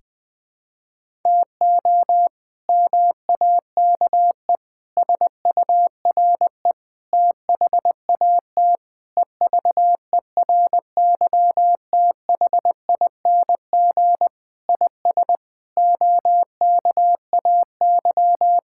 Morse code, heard as a keyed tone, about 20 wpm, 700 Hz. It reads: TO MAKE SURE THAT EVERYTHING IS OKAY
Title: Sentences with 4-Word Phrases - Spoken Beginning - 20wpm